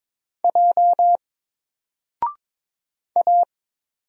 Morse code, heard as a keyed tone, about 22 wpm, 700 Hz. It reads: J A
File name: Single-Letter-Number-22wpm